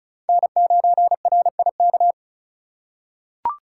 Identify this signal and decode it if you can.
Morse code, heard as a keyed tone, about 34 wpm, 700 Hz. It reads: N9RIK
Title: USA Call Signs - 5 Characters - 4x Character Spacing - 35wpm